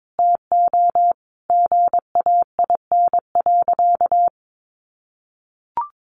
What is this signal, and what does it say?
Morse code, heard as a keyed tone, about 22 wpm, 700 Hz. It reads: TO GAIN.
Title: Sentences from Top 1100 Words 22wpm